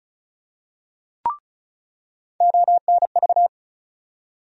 Morse code, heard as a keyed tone, about 34 wpm, 700 Hz. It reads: ONV
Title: USA Call Sign Postfixes - 3 Letters 35wpm